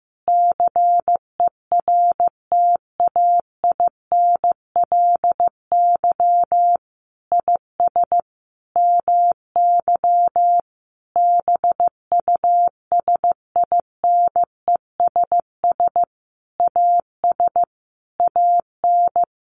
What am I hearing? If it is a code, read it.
Morse code, heard as a keyed tone, about 15 wpm, 700 Hz. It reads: CERTAINLY IS MY BUSINESS AS AN
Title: Sentences from Top 1200 Words 15wpm